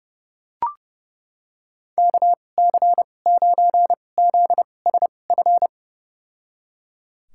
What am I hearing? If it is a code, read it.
Morse code, heard as a keyed tone, about 30 wpm, spaced as for 22 wpm, 700 Hz. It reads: KC9ZSF
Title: USA Call Signs - 6 Characters - 2x Character Spacing - 30wpm